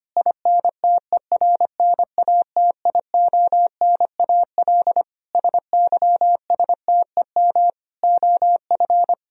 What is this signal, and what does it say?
Morse code, heard as a keyed tone, about 25 wpm, 700 Hz. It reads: INTERNATIONAL SYSTEM OF
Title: Sentences from Top 1000 Words - Review Part 4 - 25wpm